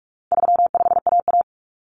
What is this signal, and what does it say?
Morse code, heard as a keyed tone, about 44 wpm, 700 Hz. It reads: 35AA
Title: Common Cut Numbers - 4-Digit 45wpm